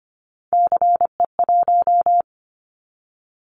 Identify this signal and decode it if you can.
Morse code, heard as a keyed tone, about 25 wpm, 700 Hz. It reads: CE1
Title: Callsign Prefixes for Top 100 DXCC - 25wpm